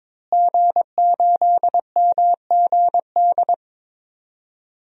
Morse code, heard as a keyed tone, about 22 wpm, 700 Hz. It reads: G8MGD
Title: DXCC-100 Call Signs - 5 Characters - Repeated 2x - 22wpm